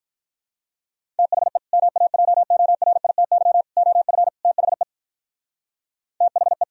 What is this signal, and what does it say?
Morse code, heard as a keyed tone, about 43 wpm, 700 Hz. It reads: THE MAJORITY OF THE THE
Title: Speed-Racing - Sets of 4 Words 35wpm